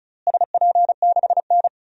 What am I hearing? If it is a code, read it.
Morse code, heard as a keyed tone, about 34 wpm, 700 Hz. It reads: SP6N
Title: DXCC-100 Call Signs - 4 Characters - Repeated 2x - 35wpm